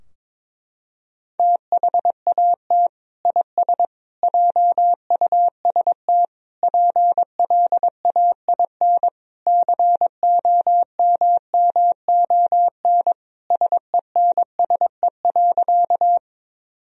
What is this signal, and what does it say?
Morse code, heard as a keyed tone, about 22 wpm, 700 Hz. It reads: THAT IS JUST PLAIN COMMON SENSE.
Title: Sentences from Top 1000 Words - Review Part 4 - 22wpm